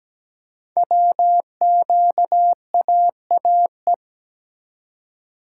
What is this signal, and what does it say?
Morse code, heard as a keyed tone, about 17 wpm, 700 Hz. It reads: WQAAE
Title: Code Groups - 5 Letter:Numbers - 17wpm